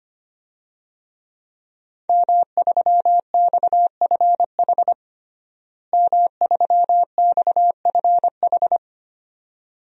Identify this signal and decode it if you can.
Morse code, heard as a keyed tone, about 25 wpm, 700 Hz. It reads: M3XFH M3XFH
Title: DXCC-100 Call Signs - 5 Characters - Repeated 2x - 25wpm